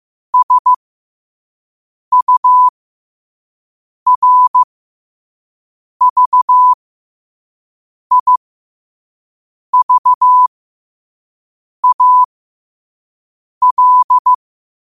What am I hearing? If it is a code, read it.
Morse code, heard as a keyed tone, about 15 wpm, spaced as for 5 wpm, 1000 Hz. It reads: SURVIVAL
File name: qotdfiles / 5wpm